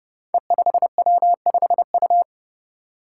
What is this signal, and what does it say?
Morse code, heard as a keyed tone, about 30 wpm, 700 Hz. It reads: E5W5U
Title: Five Challenging Letter-Number Combinations 30wpm